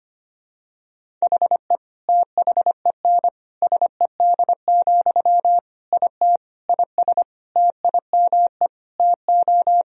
Morse code, heard as a keyed tone, about 25 wpm, 700 Hz. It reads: HE THEN SED, IT IS TIME TO
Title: Sentences from Top 300 Words and Abbreviations 25wpm